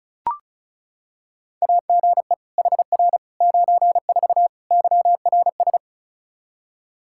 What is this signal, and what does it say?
Morse code, heard as a keyed tone, about 34 wpm, 700 Hz. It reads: AGE HR 94 YRS
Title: K1USN SST QSO Practice 35wpm